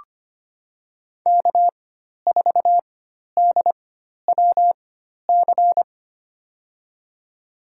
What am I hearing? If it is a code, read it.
Morse code, heard as a keyed tone, about 25 wpm, spaced as for 12 wpm, 700 Hz. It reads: K4DWC
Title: USA Call Signs - 5 Characters - 4x Character Spacing - 25wpm